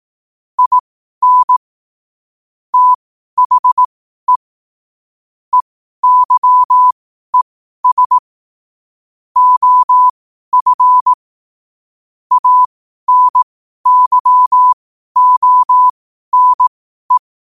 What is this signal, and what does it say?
Morse code, heard as a keyed tone, about 18 wpm, spaced as for 12 wpm, 1000 Hz. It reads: IN THE EYES OF ANYONE
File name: qotdfiles / 12wpm